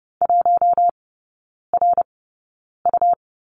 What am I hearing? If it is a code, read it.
Morse code, heard as a keyed tone, about 30 wpm, spaced as for 9 wpm, 700 Hz. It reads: 1RU
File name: Lesson 19e - Letter w - Three Characters 30wpm